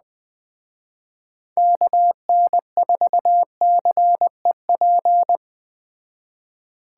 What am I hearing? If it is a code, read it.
Morse code, heard as a keyed tone, about 21 wpm, 700 Hz. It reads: KN4CEP